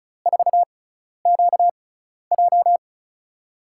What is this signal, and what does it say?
Morse code, heard as a keyed tone, about 36 wpm, spaced as for 12 wpm, 700 Hz. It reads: 4QJ
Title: USA Call Signs - 5 Characters - 6x Character Spacing - 35wpm